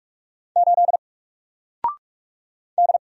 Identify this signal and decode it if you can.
Morse code, heard as a keyed tone, about 44 wpm, 700 Hz. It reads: 8 D
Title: Single Character (Letter:Number:Punctuation:Prosign) 45wpm